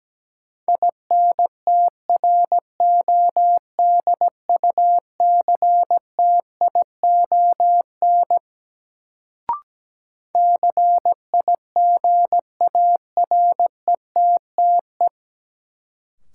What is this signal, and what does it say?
Morse code, heard as a keyed tone, about 17 wpm, 700 Hz. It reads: INTRODUCTION CIGARETTE
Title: Sentences from Top 1000 Words - New Word Drill - 17wpm